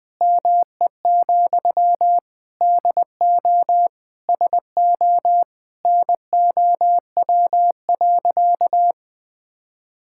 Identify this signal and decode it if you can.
Morse code, heard as a keyed tone, about 20 wpm, 700 Hz. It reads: ME, DO SO NOW.
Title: Sentences from Top 1600 Words 20wpm